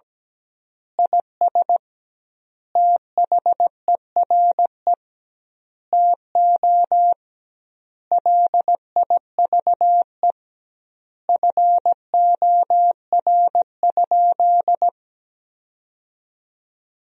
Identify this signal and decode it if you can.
Morse code, heard as a keyed tone, about 17 wpm, 700 Hz. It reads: IS THERE TO LIVE FOR?
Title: 2x Word-Spacing - Sentences from Top 200 Words - 17wpm